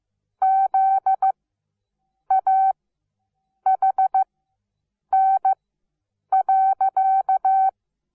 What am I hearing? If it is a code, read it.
Morse code, heard as a keyed tone, about 15 wpm, spaced as for 7 wpm, 770 Hz. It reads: ZAHN.
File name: Onlinelehrgang / morsen / DJ4UF Morsen17